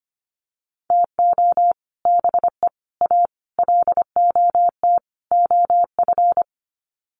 Morse code, heard as a keyed tone, about 25 wpm, 700 Hz. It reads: TO BE A LOT OF
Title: Sets of 5 Words Encore 25wpm